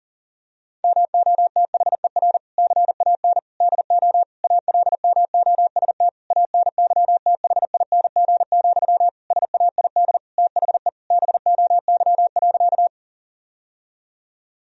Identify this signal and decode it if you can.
Morse code, heard as a keyed tone, about 40 wpm, 700 Hz. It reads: MOTHER CAN DO ALMOST ANYTHING, SAID THE BOY.
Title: Sentences from Top 1000 Words - Review Part 4 - 40wpm